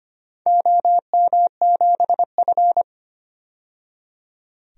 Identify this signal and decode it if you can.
Morse code, heard as a keyed tone, about 25 wpm, 700 Hz. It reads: OM7F
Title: DXCC-100 Call Signs - 4 Characters - 25wpm